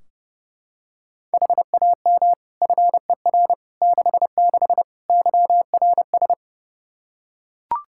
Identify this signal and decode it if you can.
Morse code, heard as a keyed tone, about 30 wpm, 700 Hz. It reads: HAM FER 66 YRS
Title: Practice QSO - Exchange 3 - More Info - 30wpm